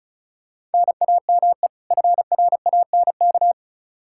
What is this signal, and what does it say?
Morse code, heard as a keyed tone, about 36 wpm, 700 Hz. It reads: NAME FRANK
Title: Practice QSO - Exchange 1 - Beginning - 35wpm